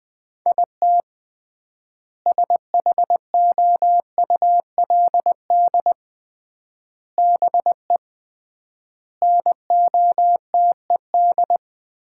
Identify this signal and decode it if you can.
Morse code, heard as a keyed tone, about 20 wpm, 700 Hz. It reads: IT SHOULD BE NOTED
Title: 3x Word-Spacing - Sets of 5 Words - 20wpm